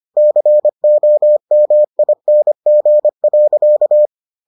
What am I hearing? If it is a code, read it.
Morse code, heard as a keyed tone, about 25 wpm, 600 Hz. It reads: COMING.